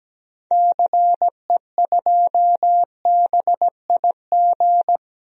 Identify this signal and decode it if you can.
Morse code, heard as a keyed tone, about 17 wpm, 700 Hz. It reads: CE2BIG
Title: DXCC-100 Call Signs - 6 Characters - 17wpm